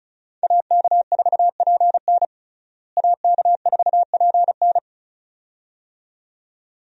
Morse code, heard as a keyed tone, about 34 wpm, 700 Hz. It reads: AK4PN AK4PN
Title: USA Call Signs - 5 Characters - Repeated 2x - 35wpm